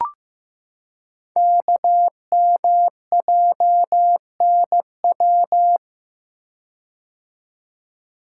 Morse code, heard as a keyed tone, about 15 wpm, 700 Hz. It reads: KMJNW
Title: Code Groups - 5 Letters - 15wpm